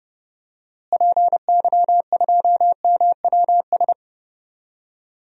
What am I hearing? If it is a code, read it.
Morse code, heard as a keyed tone, about 30 wpm, 700 Hz. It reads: PY2MWS
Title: DXCC-100 Call Signs - 6 Characters - Repeated 2x - 30wpm